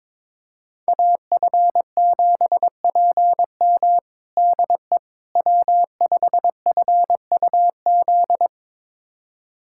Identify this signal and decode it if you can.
Morse code, heard as a keyed tone, about 22 wpm, 700 Hz. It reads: AF7PM DE W5FUZ
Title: USA Callsign DE Callsign - 5 Characters - 22wpm